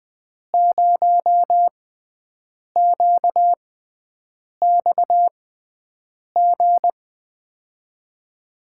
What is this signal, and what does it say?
Morse code, heard as a keyed tone, about 20 wpm, spaced as for 7 wpm, 700 Hz. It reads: 0QXG
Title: USA Call Signs - 5 Characters - 6x Character Spacing - 20wpm